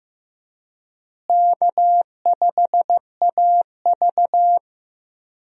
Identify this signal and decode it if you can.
Morse code, heard as a keyed tone, about 15 wpm, 700 Hz. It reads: K5AV
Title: USA Call Signs - 4 Characters - Repeated 3x - 15wpm